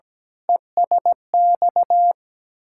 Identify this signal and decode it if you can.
Morse code, heard as a keyed tone, about 17 wpm, 690 Hz. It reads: ESX
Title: Ontario QSO Party Exchange 17wpm